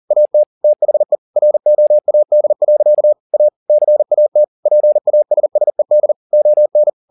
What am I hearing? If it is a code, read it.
Morse code, heard as a keyed tone, about 40 wpm, 600 Hz. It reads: AT THE ROAD. A CAT PASSED ON
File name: the old man and the sea 2 40wpm